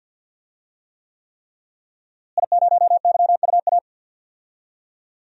Morse code, heard as a keyed tone, about 52 wpm, 700 Hz. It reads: I0YUA
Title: DXCC-100 Call Signs - 5 Characters - Repeated 3x - 50wpm